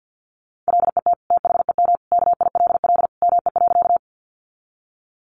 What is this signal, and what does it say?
Morse code, heard as a keyed tone, about 50 wpm, 700 Hz. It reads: LET THEM KILL ME.
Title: Sentences from Top 1400 Words 50wpm